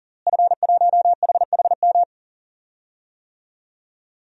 Morse code, heard as a keyed tone, about 40 wpm, 700 Hz. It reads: F1HHM
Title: DXCC-100 Call Signs - 5 Characters - Repeated 2x - 40wpm